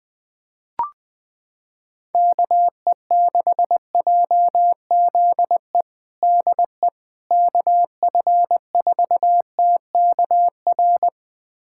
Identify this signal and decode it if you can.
Morse code, heard as a keyed tone, about 20 wpm, 700 Hz. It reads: KE6JZE DE KF4TKR
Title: USA Callsign DE Callsign - 6 Characters - 20wpm